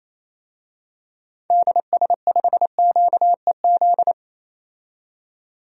Morse code, heard as a keyed tone, about 28 wpm, 700 Hz. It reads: DS5QEZ
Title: DXCC-100 Call Signs - 6 Characters - 28wpm